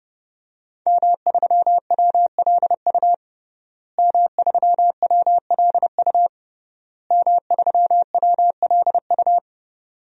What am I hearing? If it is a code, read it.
Morse code, heard as a keyed tone, about 30 wpm, 700 Hz. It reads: M3WLU M3WLU M3WLU
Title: DXCC-100 Call Signs - 5 Characters - Repeated 3x - 30wpm